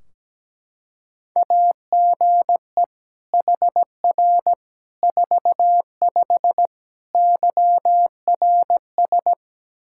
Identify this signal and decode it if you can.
Morse code, heard as a keyed tone, about 17 wpm, 700 Hz. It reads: AGE HR 45 YRS
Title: K1USN SST QSO Practice 17wpm